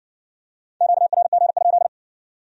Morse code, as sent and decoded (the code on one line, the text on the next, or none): --.- .-. --. ..--..
QRG?